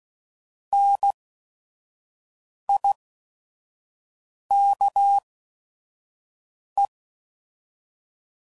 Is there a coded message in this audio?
Yes: Morse code from 0.7 to 6.9 s, heard as a keyed tone, about 16 wpm, spaced as for 5 wpm, 800 Hz. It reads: NIKE